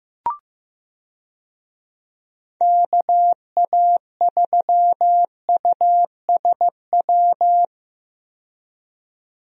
Morse code, heard as a keyed tone, about 15 wpm, 700 Hz. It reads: KA3USW
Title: USA Call Signs - 6 Characters - 15wpm